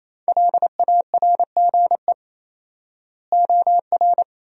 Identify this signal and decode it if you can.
Morse code, heard as a keyed tone, about 28 wpm, 700 Hz. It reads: LARGE OR
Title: 4x Word-Spacing - Sentences from Top 200 Words - 28wpm